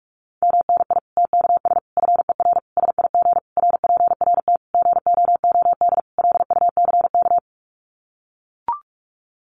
Morse code, heard as a keyed tone, about 44 wpm, 700 Hz. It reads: MNI TKS FER SIG RPRT GOOD LUCK
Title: Common QSO Phrases 45wpm